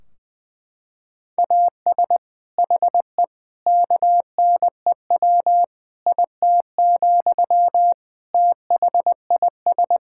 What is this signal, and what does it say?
Morse code, heard as a keyed tone, about 20 wpm, 700 Hz. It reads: AS HE KNEW IT, THIS
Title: Sentences from Top 1600 Words 20wpm